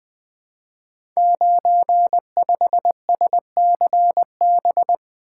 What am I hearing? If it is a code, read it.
Morse code, heard as a keyed tone, about 20 wpm, 700 Hz. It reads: 95SCB
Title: Code Groups - 5 Letter:Numbers - 20wpm